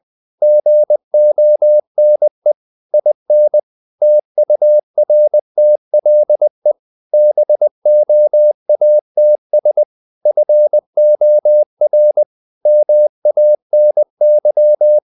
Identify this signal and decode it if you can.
Morse code, heard as a keyed tone, about 20 wpm, 600 Hz. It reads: GONE IN TURTLE BOATS FOR MANY
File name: the old man and the sea / the old man and the sea part 1 20wpm